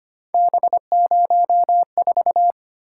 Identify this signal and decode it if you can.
Morse code, heard as a keyed tone, about 29 wpm, 700 Hz. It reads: B04
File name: Speed-Racing - Three Letter-Number Combinations - 25wpm